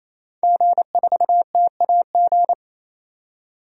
Morse code, heard as a keyed tone, about 28 wpm, 700 Hz. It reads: G4TAG
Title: DXCC-100 Call Signs - 5 Characters - 28wpm